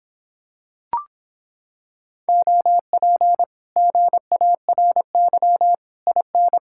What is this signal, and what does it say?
Morse code, heard as a keyed tone, about 26 wpm, 700 Hz. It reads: OP GARY IN